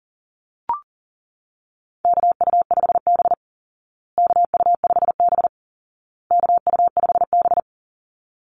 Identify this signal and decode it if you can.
Morse code, heard as a keyed tone, about 40 wpm, 700 Hz. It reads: KU5B KU5B KU5B
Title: Call Signs Repeated 2x - 40wpm